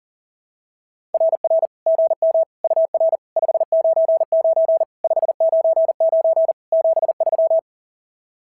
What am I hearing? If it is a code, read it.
Morse code, heard as a keyed tone, about 37 wpm, 660 Hz. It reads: RR GM UR 599 599 73
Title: SOTA QSO Practice 40wpm